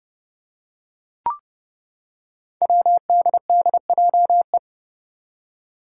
Morse code, heard as a keyed tone, about 31 wpm, 700 Hz. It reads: WDDJE
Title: Five Challenging Letter-Number Combinations 30wpm